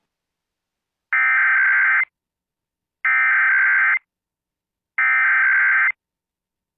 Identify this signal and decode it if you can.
Emergency Alert System (SAME) tones (header bursts), heard as harsh buzzing bursts. It is heard from 1.1 to 5.9 s.